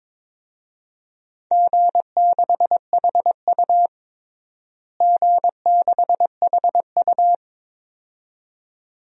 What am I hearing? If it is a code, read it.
Morse code, heard as a keyed tone, about 22 wpm, 700 Hz. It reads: G6HU G6HU
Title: DXCC-100 Call Signs - 4 Characters - Repeated 2x - 22wpm